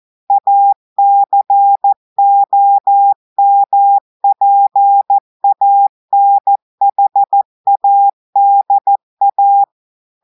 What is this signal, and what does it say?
Morse code, heard as a keyed tone, about 14 wpm, 800 Hz. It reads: ACOMPANHADA
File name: audios / portugais / portugais texte en clair 2 vitesse 14 mots minutes